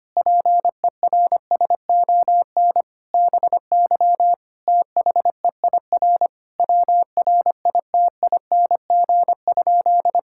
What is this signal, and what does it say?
Morse code, heard as a keyed tone, about 25 wpm, 700 Hz. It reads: PERSON BY THEIR WRITING?
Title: Sentences from Top 1000 Words - Review Part 6 - 25wpm